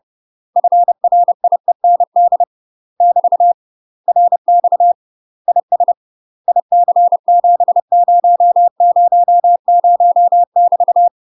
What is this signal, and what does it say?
Morse code, heard as a keyed tone, about 30 wpm, 700 Hz. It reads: FRIEND <BT> RX IS IC7000 <BT>
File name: UK MASTER QSOs TRAINING / 30WPM / UK MASTER QSOs 0017